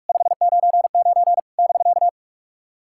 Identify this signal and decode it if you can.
Morse code, heard as a keyed tone, about 44 wpm, 690 Hz. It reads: 599 <BK>
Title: SOTA QSO Practice 45wpm